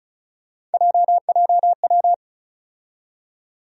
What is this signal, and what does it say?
Morse code, heard as a keyed tone, about 36 wpm, 700 Hz. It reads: JJW
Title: USA Call Sign Postfixes - 3 Letters 35wpm